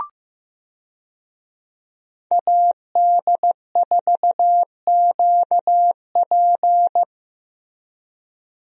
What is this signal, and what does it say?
Morse code, heard as a keyed tone, about 15 wpm, 700 Hz. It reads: AD4QP
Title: USA Call Signs - 5 Characters - Repeated 3x - 15wpm